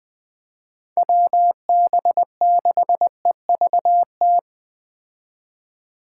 Morse code, heard as a keyed tone, about 20 wpm, 700 Hz. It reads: WB6EVT
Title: Speed Racing - USA 2x3 Call Signs 17wpm